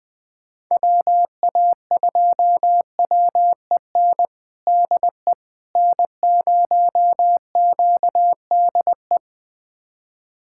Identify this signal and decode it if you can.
Morse code, heard as a keyed tone, about 21 wpm, 690 Hz. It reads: WA2WEN DE N0QDE
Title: USA Callsign DE Callsign 20wpm